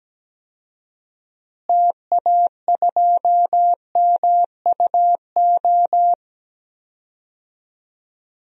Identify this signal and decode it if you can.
Morse code, heard as a keyed tone, about 17 wpm, 700 Hz. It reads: TA2MUO
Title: DXCC-100 Call Signs - 6 Characters - Repeated 3x - 17wpm